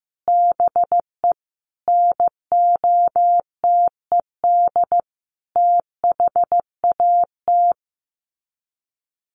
Morse code, heard as a keyed tone, about 15 wpm, 700 Hz. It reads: BE NOTED THAT
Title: Sets of 5 Words 15wpm